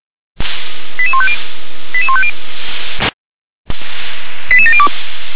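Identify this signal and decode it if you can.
Five-tone selective call (ZVEI), heard as quick runs of five tones, tones of about 70 ms. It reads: 88150 88150 88702
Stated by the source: ZVEI five-tone selective call 88702 received by SDR.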